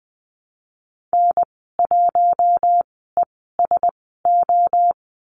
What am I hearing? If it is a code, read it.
Morse code, heard as a keyed tone, about 20 wpm, spaced as for 15 wpm, 700 Hz. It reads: N1ESO
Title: 076 Lesson 13g - Letter L - Call Signs 20wpm